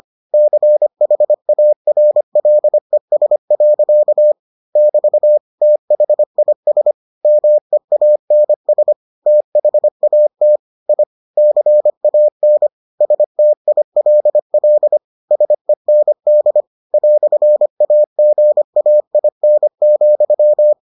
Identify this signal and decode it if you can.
Morse code, heard as a keyed tone, about 25 wpm, 600 Hz. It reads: CHARLES. <BT> THIS MEANS THAT I CAN STILL SEND "AGAIN,